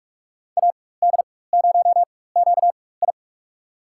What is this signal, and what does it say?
Morse code, heard as a keyed tone, about 46 wpm, spaced as for 21 wpm, 700 Hz. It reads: AD0QI
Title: USA Call Signs - 5 Characters - 4x Character Spacing - 45wpm